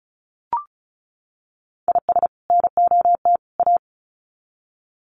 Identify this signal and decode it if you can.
Morse code, heard as a keyed tone, about 34 wpm, 700 Hz. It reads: IS NOT A
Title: Sets of 3 Words 35wpm